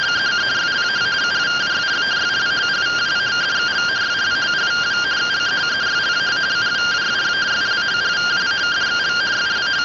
Two-tone teleprinter signal, about 170 Hz shift.